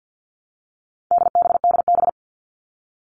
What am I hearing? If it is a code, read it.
Morse code, heard as a keyed tone, about 50 wpm, 700 Hz. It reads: DBDB
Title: All Cut Numbers - 4-Digit 50wpm